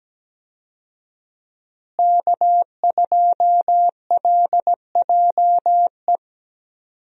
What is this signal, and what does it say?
Morse code, heard as a keyed tone, about 17 wpm, 700 Hz. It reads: K2LJE
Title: USA Call Signs - 5 Characters - Repeated 3x - 17wpm